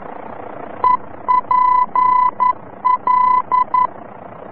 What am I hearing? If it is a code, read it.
Morse code, heard as a keyed tone, about 11 wpm, 1000 Hz. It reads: EPL
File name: france / mx / EPL-VOR-F